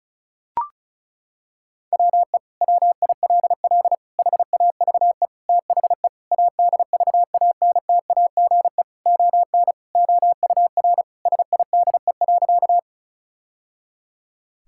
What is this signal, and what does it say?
Morse code, heard as a keyed tone, about 34 wpm, 700 Hz. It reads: WE WILL HAVE THE ADVANTAGE ON OUR SIDE.
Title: Sentences from Top 1000 Words - Review Part 6 - 35wpm